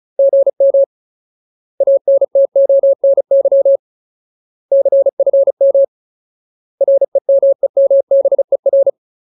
Morse code, heard as a keyed tone, about 34 wpm, 550 Hz. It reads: GM ANTONY CFM REMEMBER
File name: Ultimate Word List / 35WPM Ultimate